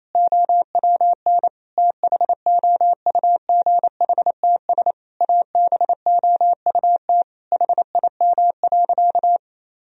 Morse code, heard as a keyed tone, about 28 wpm, 700 Hz. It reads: OWN THOUGHTS ABOUT HIM.
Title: Sentences from Top 1000 Words - Review Part 1 - 28wpm